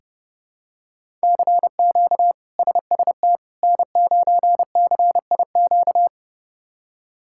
Morse code, heard as a keyed tone, about 30 wpm, 700 Hz. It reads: CQ SST N9CIQ
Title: K1USN SST QSO Practice - 30wpm